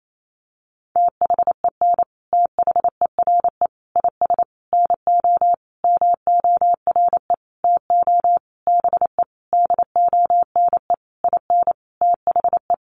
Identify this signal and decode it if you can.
Morse code, heard as a keyed tone, about 28 wpm, 700 Hz. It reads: THEN THERE IS NO MORE TO BE DONE IN THE
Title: Sentences from Top 700 Words 28wpm